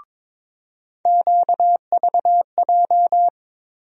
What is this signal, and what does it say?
Morse code, heard as a keyed tone, about 22 wpm, 700 Hz. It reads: QVJ
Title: USA Call Sign Postfixes - 3 Letters 22wpm